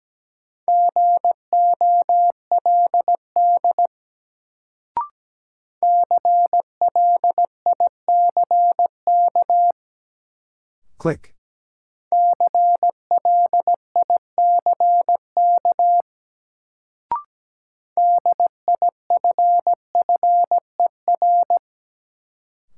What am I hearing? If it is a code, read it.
Morse code, heard as a keyed tone, about 17 wpm, 700 Hz. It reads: GOLD CLICK CLICK DIFFER
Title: Sentences from Top 1100 Words - New Word Drill Alt - 17wpm